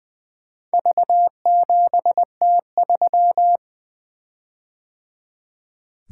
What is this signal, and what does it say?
Morse code, heard as a keyed tone, about 20 wpm, 700 Hz. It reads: V7T3
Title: Four Letter-Number Combinations - 20wpm